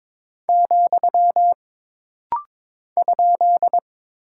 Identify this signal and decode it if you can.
Morse code, heard as a keyed tone, about 22 wpm, 700 Hz. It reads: , ?
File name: practice / 22wpm / Single Punctuation-Prosign - Rapid-Fire - 22wpm